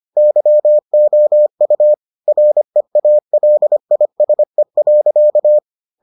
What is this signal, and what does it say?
Morse code, heard as a keyed tone, about 25 wpm, 600 Hz. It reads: YOU REALISE.